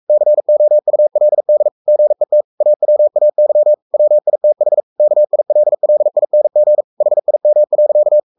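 Morse code, heard as a keyed tone, about 44 wpm, 600 Hz. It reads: COULD GET AWAY WITH KILLING HIM.
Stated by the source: Morse code at 43 wpm sending Ken Follett's novel Never in English, from CW med Gnister.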